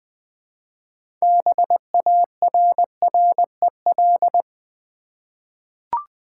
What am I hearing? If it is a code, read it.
Morse code, heard as a keyed tone, about 20 wpm, 700 Hz. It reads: BARREL
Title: 6 Letter Words - First 3 Letters Given - 20wpm